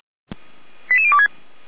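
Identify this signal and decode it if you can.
Five-tone selective call (ZVEI), heard as quick runs of five tones, tones of about 70 ms. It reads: 88026
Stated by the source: ZVEI five-tone call received with an SDR, from the Feuerwehr Orlamünde audio server.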